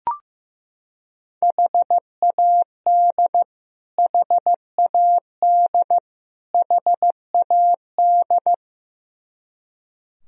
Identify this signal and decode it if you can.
Morse code, heard as a keyed tone, about 15 wpm, 700 Hz. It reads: HAD HAD HAD
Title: Top 100 Words Repeated 15wpm